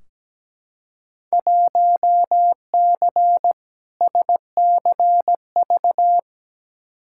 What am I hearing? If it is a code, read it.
Morse code, heard as a keyed tone, about 17 wpm, 700 Hz. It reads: 1C SCV
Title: Field Day Exchanges - 17wpm